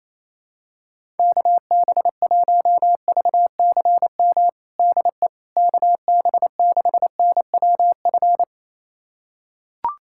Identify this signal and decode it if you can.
Morse code, heard as a keyed tone, about 28 wpm, 700 Hz. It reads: KB1VCM DE KB6NWF
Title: USA Callsign DE Callsign - 6 Characters - 28wpm